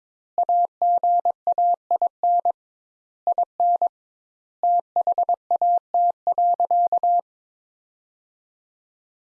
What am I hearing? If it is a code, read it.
Morse code, heard as a keyed tone, about 22 wpm, 700 Hz. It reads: AGAIN IN THAT.
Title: Sentences Top 200 Words - 22wpm